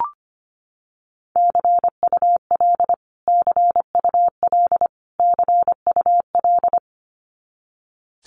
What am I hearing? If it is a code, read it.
Morse code, heard as a keyed tone, about 25 wpm, 700 Hz. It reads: CUL CUL CUL
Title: Abbreviations Repeated 3x - 25wpm